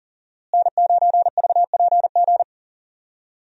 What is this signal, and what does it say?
Morse code, heard as a keyed tone, about 40 wpm, 700 Hz. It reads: N9VPG
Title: USA Call Signs - 5 Characters - Repeated 2x - 40wpm